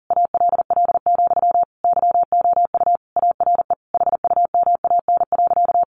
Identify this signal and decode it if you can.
Morse code, heard as a keyed tone, about 40 wpm, 700 Hz. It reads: ALL, YOU ARE HUMAN.
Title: Sentences from Top 600 Words 40wpm